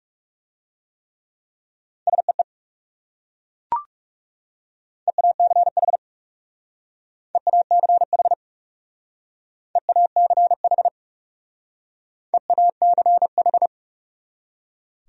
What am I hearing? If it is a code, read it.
Morse code, heard as a keyed tone, about 39 wpm, 700 Hz. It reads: SEE EACH EACH EACH EACH